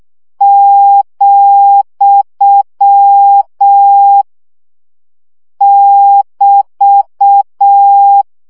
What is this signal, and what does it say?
Morse code, heard as a keyed tone, about 6 wpm, 800 Hz. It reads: , <BT>
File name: CW gyakorlas vesszo 5 baud